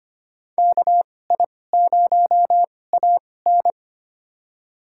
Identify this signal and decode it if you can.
Morse code, heard as a keyed tone, about 25 wpm, spaced as for 18 wpm, 700 Hz. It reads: KI0AN
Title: USA Call Signs - 5 Characters - 2x Character Spacing - 25wpm